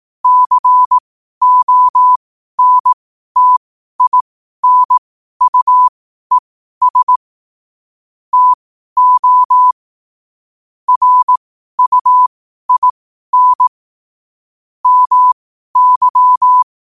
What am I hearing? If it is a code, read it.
Morse code, heard as a keyed tone, about 18 wpm, spaced as for 12 wpm, 1000 Hz. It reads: CONTINUES TO RUIN MY
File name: qotdfiles / 12wpm